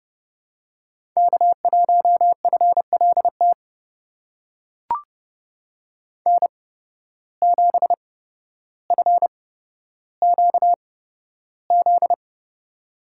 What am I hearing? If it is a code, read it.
Morse code, heard as a keyed tone, about 30 wpm, spaced as for 13 wpm, 700 Hz. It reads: K1FLT N7FQZ
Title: USA Call Signs - 5 Characters - 8x Character Spacing - 30wpm